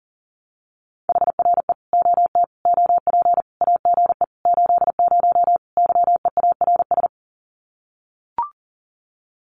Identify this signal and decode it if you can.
Morse code, heard as a keyed tone, about 40 wpm, 700 Hz. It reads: HRE OT OP AGE 80 YEARS
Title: Common QSO Phrases 40wpm